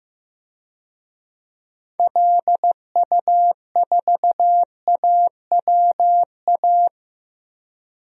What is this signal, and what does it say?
Morse code, heard as a keyed tone, about 15 wpm, 700 Hz. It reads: LU4AWA
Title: DXCC-100 Call Signs - 6 Characters - Repeated 3x - 15wpm